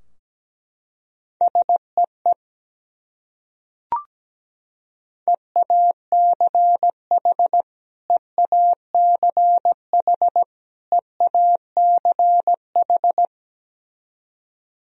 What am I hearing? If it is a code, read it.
Morse code, heard as a keyed tone, about 17 wpm, 700 Hz. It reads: SEE EACH EACH EACH
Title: Top 100 Words Repeated 17wpm